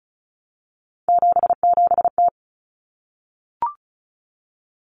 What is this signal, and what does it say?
Morse code, heard as a keyed tone, about 34 wpm, 700 Hz. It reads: 77T
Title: Common Cut Numbers - 3-Digit 35wpm